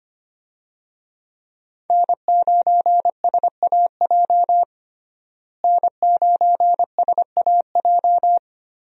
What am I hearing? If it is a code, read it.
Morse code, heard as a keyed tone, about 25 wpm, 700 Hz. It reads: N9SAJ N9SAJ
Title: USA Call Signs - 5 Characters - Repeated 3x - 25wpm